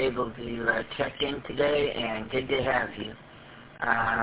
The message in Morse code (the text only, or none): none